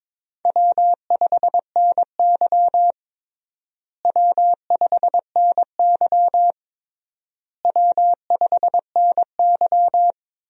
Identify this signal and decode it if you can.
Morse code, heard as a keyed tone, about 22 wpm, 700 Hz. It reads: W5NY W5NY W5NY
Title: USA Call Signs - 4 Characters - Repeated 3x - 22wpm